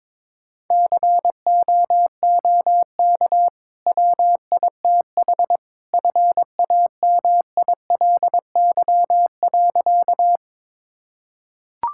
Morse code, heard as a keyed tone, about 22 wpm, 700 Hz. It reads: COOK WITH FAMILY.
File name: practice / 22wpm / Easy Sentences with Hints - 22wpm